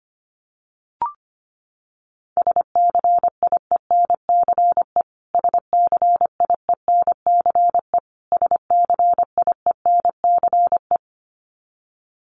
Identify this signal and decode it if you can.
Morse code, heard as a keyed tone, about 25 wpm, 700 Hz. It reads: SCIENCE SCIENCE SCIENCE
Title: Top 500 Words Repeated 25wpm